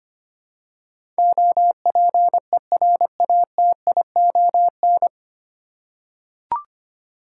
Morse code, heard as a keyed tone, about 25 wpm, 700 Hz. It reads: OPERATION
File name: Nine Letter Words - 25wpm